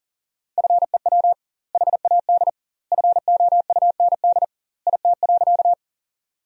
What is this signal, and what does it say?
Morse code, heard as a keyed tone, about 40 wpm, 700 Hz. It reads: FEW HAD FOUND IT.
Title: 2x Word-Spacing - Sentences from Top 300 Words - 40wpm